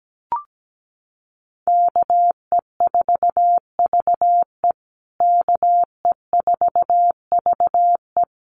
Morse code, heard as a keyed tone, about 17 wpm, 700 Hz. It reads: KE4VE KE4VE
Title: USA Call Signs Repeated 3x 17wpm